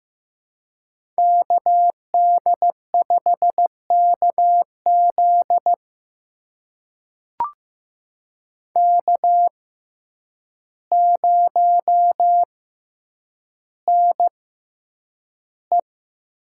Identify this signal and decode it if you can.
Morse code, heard as a keyed tone, about 15 wpm, spaced as for 8 wpm, 700 Hz. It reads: KD5KZ K0NE
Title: USA Call Signs - 6x Character Spacing - 15wpm